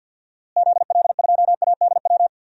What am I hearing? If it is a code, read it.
Morse code, heard as a keyed tone, about 51 wpm, 700 Hz. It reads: ZL2ADW
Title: DXCC-100 Call Signs - 6 Characters - Repeated 3x - 50wpm